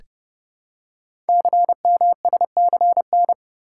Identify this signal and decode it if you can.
Morse code, heard as a keyed tone, about 30 wpm, 700 Hz. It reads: CMSCN
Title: Code Groups - 5 Letters - 30wpm